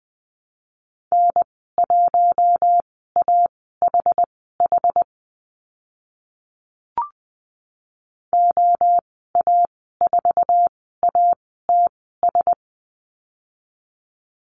Morse code, heard as a keyed tone, about 20 wpm, spaced as for 14 wpm, 700 Hz. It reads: N1AHH OA4ATS
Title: 062 Lesson 11g - Letter h - Call Signs 20wpm